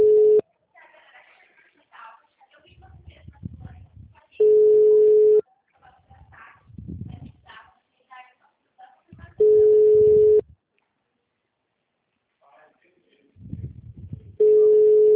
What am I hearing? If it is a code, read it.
Ringback tone (Europe, 425 Hz), with speech.
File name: out- -2009- - - .896